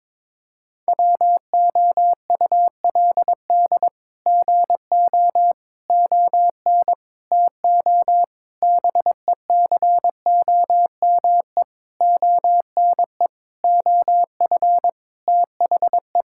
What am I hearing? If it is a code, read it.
Morse code, heard as a keyed tone, about 22 wpm, 700 Hz. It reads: WOULD GO ON TO BECOME ONE OF THE
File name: practice / 22wpm / Sets of 8 Words - 22wpm